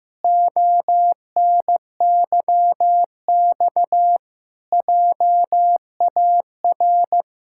Morse code, heard as a keyed tone, about 14 wpm, 700 Hz. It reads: ONYX JAR
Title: Pangram Challenge v2 - 15wpm